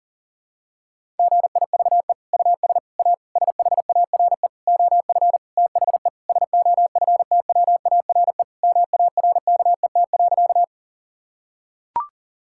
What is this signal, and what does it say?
Morse code, heard as a keyed tone, about 37 wpm, 700 Hz. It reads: GIVE US A SHARE OF THE SOFTWARE MARKET.
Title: Sentences from Top 1000 Words - Review Part 6 - 40wpm